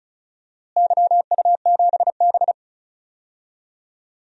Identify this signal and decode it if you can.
Morse code, heard as a keyed tone, about 34 wpm, 700 Hz. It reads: YU7B
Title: DXCC-100 Call Signs - 4 Characters - Repeated 2x - 35wpm